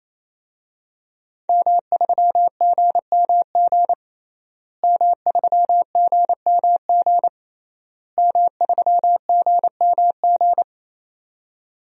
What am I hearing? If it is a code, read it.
Morse code, heard as a keyed tone, about 27 wpm, 700 Hz. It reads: M3GMG M3GMG M3GMG
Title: DXCC-100 Call Signs - 5 Characters - Repeated 3x - 28wpm